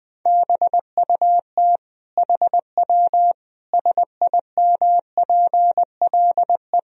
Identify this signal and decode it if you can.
Morse code, heard as a keyed tone, about 21 wpm, 700 Hz. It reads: BUT HW SIMPLE
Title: Sentences from Top 400 Words and Abbreviations 20wpm